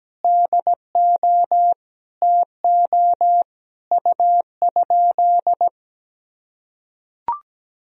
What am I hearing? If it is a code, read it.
Morse code, heard as a keyed tone, about 17 wpm, 700 Hz. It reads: DO TO U?
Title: Sentences from Top 100 Words and Abbreviations 17wpm